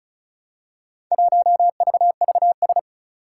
Morse code, heard as a keyed tone, about 36 wpm, 700 Hz. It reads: 1VVS
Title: Four Challenging Letter-Number Combinations 35wpm